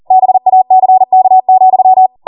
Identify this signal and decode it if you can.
Morse code, heard as a keyed tone, about 40 wpm, 750 Hz. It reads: BACK,